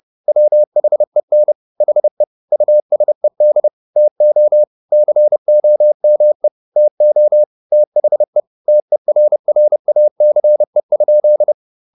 Morse code, heard as a keyed tone, about 30 wpm, 600 Hz. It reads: WHEN HE USED TO COME TO THE TERRACE?
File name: the old man and the sea / the old man and the sea 1 30wpm